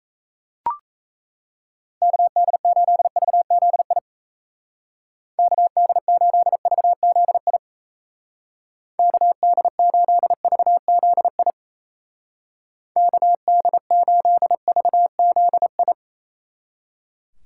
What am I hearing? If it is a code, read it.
Morse code, heard as a keyed tone, about 34 wpm, 700 Hz. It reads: KD8VZI KD8VZI KD8VZI KD8VZI
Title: Speed Racing - USA 2x3 Call Signs 28wpm